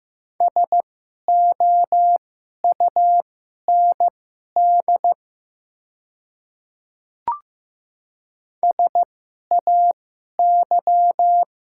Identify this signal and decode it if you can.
Morse code, heard as a keyed tone, about 15 wpm, spaced as for 11 wpm, 700 Hz. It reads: SOUND SAY
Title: 2x Character Spacing - Top 200 Words - 15wpm